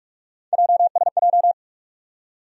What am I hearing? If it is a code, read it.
Morse code, heard as a keyed tone, about 46 wpm, 700 Hz. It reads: JSJ
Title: USA Call Sign Postfixes - 3 Letters 45wpm